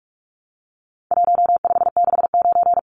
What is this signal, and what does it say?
Morse code, heard as a keyed tone, about 46 wpm, 700 Hz. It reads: 1569
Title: Numbers - Contest Style - 4-Digit 45wpm